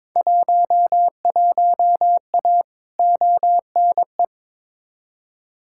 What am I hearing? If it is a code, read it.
Morse code, heard as a keyed tone, about 21 wpm, 700 Hz. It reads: 11A ONE
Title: Field Day Exchanges Encore - 22wpm